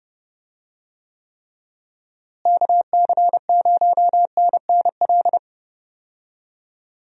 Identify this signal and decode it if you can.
Morse code, heard as a keyed tone, about 30 wpm, 700 Hz. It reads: KC0NNL
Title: USA Call Signs - 6 Characters - Repeated 3x - 30wpm